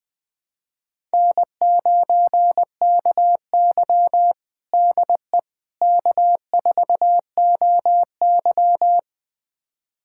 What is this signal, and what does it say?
Morse code, heard as a keyed tone, about 20 wpm, 700 Hz. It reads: N9KY DE K4OY
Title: USA Callsign DE Callsign - 4 Characters - 20wpm